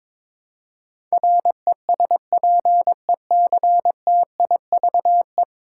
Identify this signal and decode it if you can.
Morse code, heard as a keyed tone, about 22 wpm, 700 Hz. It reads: RESPECTIVE
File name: practice / 22wpm / 10 Letter Words - First 3 Letters Given - 22wpm